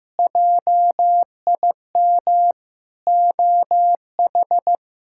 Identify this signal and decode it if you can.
Morse code, heard as a keyed tone, about 16 wpm, 700 Hz. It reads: JIM OH
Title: K1USN SST QSO Practice - 15wpm